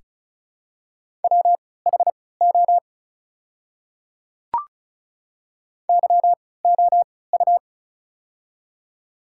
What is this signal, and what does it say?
Morse code, heard as a keyed tone, about 34 wpm, spaced as for 20 wpm, 690 Hz. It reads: WHO YOU
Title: 3x Character Spacing - Top 100 Words - 35wpm